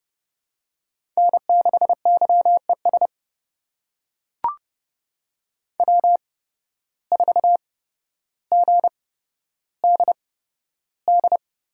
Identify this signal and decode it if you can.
Morse code, heard as a keyed tone, about 31 wpm, spaced as for 13 wpm, 700 Hz. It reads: N6YES W4GDD
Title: USA Call Signs - 5 Characters - 8x Character Spacing - 30wpm